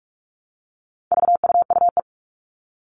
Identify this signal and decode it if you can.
Morse code, heard as a keyed tone, about 44 wpm, 700 Hz. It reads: VUUE
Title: All Cut Numbers - 4-Digit 45wpm